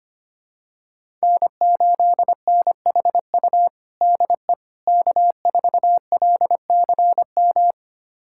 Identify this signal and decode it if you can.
Morse code, heard as a keyed tone, about 25 wpm, 700 Hz. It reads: N8NHU DE K4LCM
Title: USA Callsign DE Callsign - 5 Characters - 25wpm